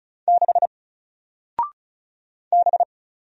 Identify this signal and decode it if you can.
Morse code, heard as a keyed tone, about 35 wpm, 700 Hz. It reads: 6 B
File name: Single-Letter-Number-35wpm